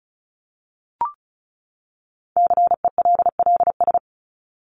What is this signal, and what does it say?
Morse code, heard as a keyed tone, about 34 wpm, 700 Hz. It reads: CELLS
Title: ES:S Suffix - Top 100 Words with Base Words 35wpm